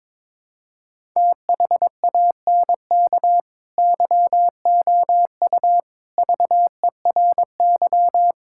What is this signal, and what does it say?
Morse code, heard as a keyed tone, about 22 wpm, 700 Hz. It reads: THANK YOU VERY
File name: practice / 22wpm / Build A Phrase - 5-Word Phrases - 22wpm